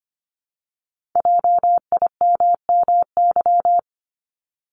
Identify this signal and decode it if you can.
Morse code, heard as a keyed tone, about 25 wpm, 700 Hz. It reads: JIMMY
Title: Multi-letter Phonemes Repeated 3x 25wpm